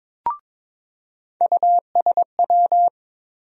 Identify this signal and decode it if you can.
Morse code, heard as a keyed tone, about 22 wpm, 700 Hz. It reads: USW
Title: USA Call Sign Postfixes - 3 Letters 22wpm